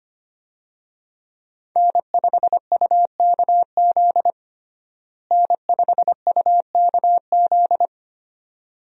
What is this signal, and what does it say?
Morse code, heard as a keyed tone, about 25 wpm, 700 Hz. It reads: N5UKZ N5UKZ
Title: USA Call Signs - 5 Characters - Repeated 2x - 25wpm